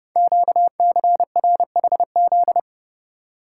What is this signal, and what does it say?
Morse code, heard as a keyed tone, about 30 wpm, 690 Hz. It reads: QCRHZ
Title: Code Groups - 5 Letters - 30wpm